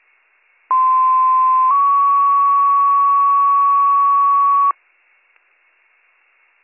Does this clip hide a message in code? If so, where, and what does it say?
Yes: two-tone page from 0.7 to 4.7 s, heard as a one-second tone, then a three-second tone, about 1040 then 1120 Hz.